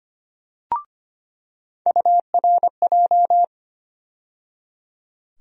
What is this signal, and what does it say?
Morse code, heard as a keyed tone, about 25 wpm, 700 Hz. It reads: URJ
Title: USA Call Sign Postfixes - 3 Letters 25wpm